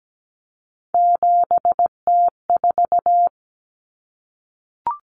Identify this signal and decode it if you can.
Morse code, heard as a keyed tone, about 17 wpm, 700 Hz. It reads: 7T4
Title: Common Cut Numbers - 3-Digit 17wpm